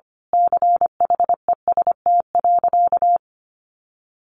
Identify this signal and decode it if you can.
Morse code, heard as a keyed tone, about 25 wpm, 700 Hz. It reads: CHEST.
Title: A Sound Of Thunder 25wpm